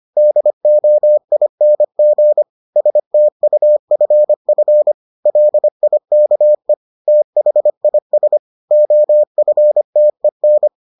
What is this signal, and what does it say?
Morse code, heard as a keyed tone, about 25 wpm, 600 Hz. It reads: DOING STUFF LIKE THIS OFTEN